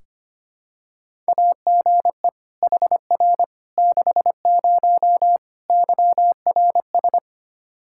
Morse code, heard as a keyed tone, about 25 wpm, 700 Hz. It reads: AGE HR 60 YRS
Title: Practice QSO - Exchange 3 - More Info - 25wpm